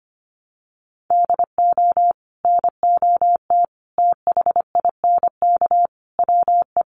Morse code, heard as a keyed tone, about 25 wpm, 700 Hz. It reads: DO NOT THINK WE
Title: Sets of 5 Words Encore 25wpm